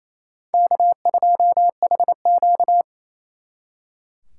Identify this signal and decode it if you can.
Morse code, heard as a keyed tone, about 35 wpm, 700 Hz. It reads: K2HQ